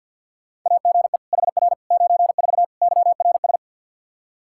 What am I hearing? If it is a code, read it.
Morse code, heard as a keyed tone, about 51 wpm, 700 Hz. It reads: AGE HR 94 YRS
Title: Practice QSO - Exchange 3 - More Info - 50wpm